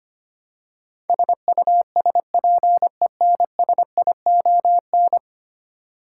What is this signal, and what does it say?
Morse code, heard as a keyed tone, about 25 wpm, 700 Hz. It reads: SUSPENSION
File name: Ten Letter Words - 25wpm